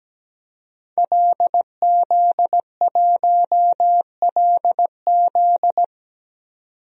Morse code, heard as a keyed tone, about 17 wpm, 700 Hz. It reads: LZ1LZ
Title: DXCC-100 Call Signs - 5 Characters - 17wpm